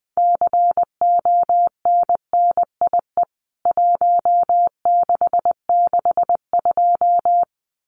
Morse code, heard as a keyed tone, about 20 wpm, 700 Hz. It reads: CONNIE 1662
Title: CWT Exchange 20wpm